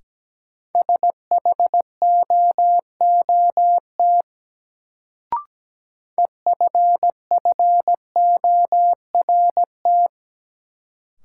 Morse code, heard as a keyed tone, about 17 wpm, 700 Hz. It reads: SHOOT EFFORT
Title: Sentences from Top 1200 Words - New Word Drill - 17wpm